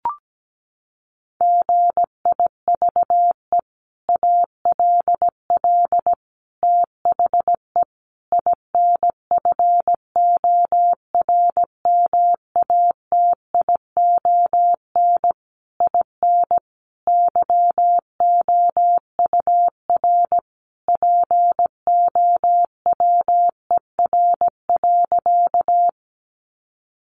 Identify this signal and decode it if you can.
Morse code, heard as a keyed tone, about 17 wpm, 700 Hz. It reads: GIVE ALL THE INFORMATION IN YOUR POWER.
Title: Sentences from Top 600 Words 17wpm